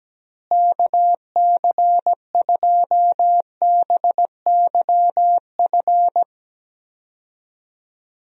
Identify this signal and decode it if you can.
Morse code, heard as a keyed tone, about 17 wpm, 700 Hz. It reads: KC2BYF
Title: USA Call Signs - 6 Characters - Repeated 3x - 17wpm